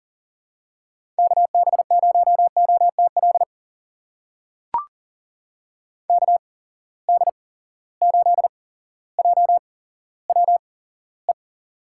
Morse code, heard as a keyed tone, about 40 wpm, spaced as for 17 wpm, 700 Hz. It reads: KB0OTL KD8JWE
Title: USA Call Signs - 6 Characters - 8x Character Spacing - 40wpm